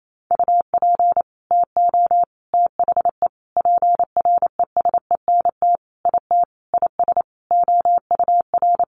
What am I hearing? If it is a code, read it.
Morse code, heard as a keyed tone, about 28 wpm, 700 Hz. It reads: UP TO THE PRESENT IT IS OUR
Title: Sentences from Top 800 Words 28wpm